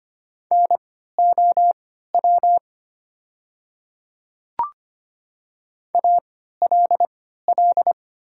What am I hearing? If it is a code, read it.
Morse code, heard as a keyed tone, about 25 wpm, spaced as for 14 wpm, 700 Hz. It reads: NOW ALL
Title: 3x Character Spacing - Top 500 Words - 25wpm